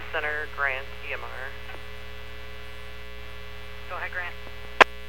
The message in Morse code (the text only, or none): none